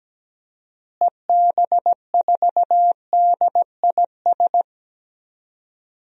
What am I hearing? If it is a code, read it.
Morse code, heard as a keyed tone, about 17 wpm, 700 Hz. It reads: EB4DIS
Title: DXCC-100 Call Signs - 6 Characters - Repeated 3x - 17wpm